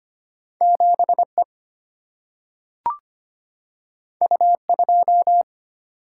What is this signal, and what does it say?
Morse code, heard as a keyed tone, about 25 wpm, 700 Hz. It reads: 7E U2
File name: Two Letter-Number Combinations - 25wpm